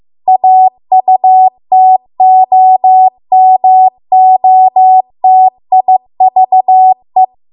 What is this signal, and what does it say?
Morse code, heard as a keyed tone, about 15 wpm, 750 Hz. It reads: AUTOMOTIVE